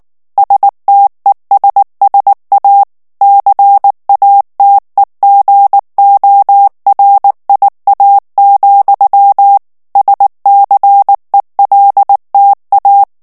Morse code, heard as a keyed tone, about 19 wpm, 800 Hz. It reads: STESSA CATEGORIA, SCELTA
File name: morse mp3 / ITALIEN texte clair 2 vitesse 18 mots minute